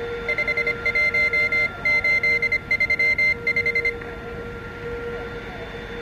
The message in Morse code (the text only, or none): 51835